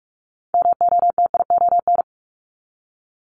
Morse code, heard as a keyed tone, about 46 wpm, 700 Hz. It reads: MOTION
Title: Multi-letter Phonemes 45wpm